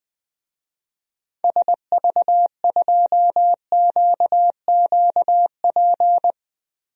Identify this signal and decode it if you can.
Morse code, heard as a keyed tone, about 20 wpm, 700 Hz. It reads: SV2QQP
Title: DXCC-100 Call Signs - 6 Characters - 20wpm